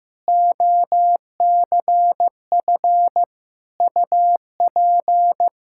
Morse code, heard as a keyed tone, about 16 wpm, 700 Hz. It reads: OCF UP
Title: Practice QSO - Exchange 2 - Info - 15wpm